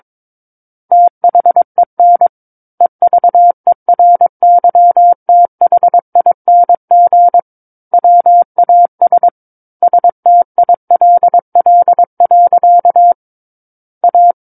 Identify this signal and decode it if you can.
Morse code, heard as a keyed tone, about 22 wpm, 700 Hz. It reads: THEN EVERYTHING WAS STILL. A
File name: Book One / 13-Chapter 13 - 22 WPM